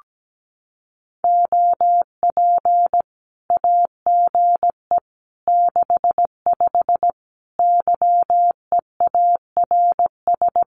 Morse code, heard as a keyed tone, about 17 wpm, 700 Hz. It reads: OP AGE 65 YEARS
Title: Common QSO Phrases 17wpm